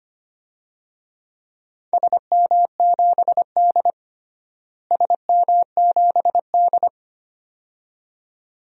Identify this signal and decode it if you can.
Morse code, heard as a keyed tone, about 25 wpm, 700 Hz. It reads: SM7D SM7D
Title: DXCC-100 Call Signs - 4 Characters - Repeated 2x - 25wpm